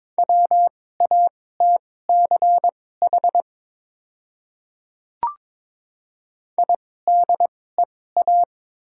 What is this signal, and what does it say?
Morse code, heard as a keyed tone, about 22 wpm, spaced as for 16 wpm, 700 Hz. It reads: WATCH IDEA
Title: 2x Character Spacing - Top 300 Words - 22wpm